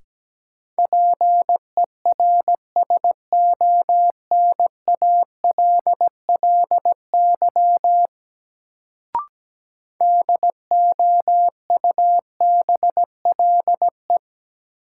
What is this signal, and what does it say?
Morse code, heard as a keyed tone, about 17 wpm, 700 Hz. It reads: PERSONALLY DOUBLE
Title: Sentences from Top 1400 Words - New Word Drill Alt - 17wpm